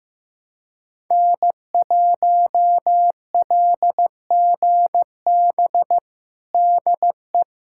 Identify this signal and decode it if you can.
Morse code, heard as a keyed tone, about 15 wpm, 700 Hz. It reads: N1LGB DE
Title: USA Callsign DE Callsign - 5 Characters - 15wpm